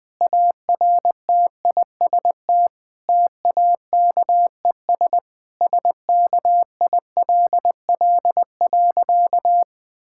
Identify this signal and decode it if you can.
Morse code, heard as a keyed tone, about 19 wpm, 700 Hz. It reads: ARTIST TAKES SKILL.
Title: Sentences with 3-Word Phrases - 20wpm